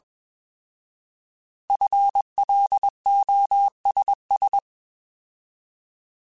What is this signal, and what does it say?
Morse code, heard as a keyed tone, about 21 wpm, 800 Hz. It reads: FLOSS
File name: Alphabetical / Alphabetical mp3s / Fs @20